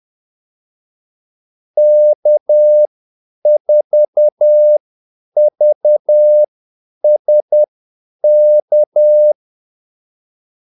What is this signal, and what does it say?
Morse code, heard as a keyed tone, about 10 wpm, spaced as for 8 wpm, 600 Hz. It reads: K4VSK
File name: Callsigns-10-8wpm